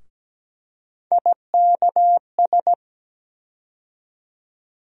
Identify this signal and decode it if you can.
Morse code, heard as a keyed tone, about 17 wpm, 700 Hz. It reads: IKS
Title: USA Call Sign Postfixes - 3 Letters 17wpm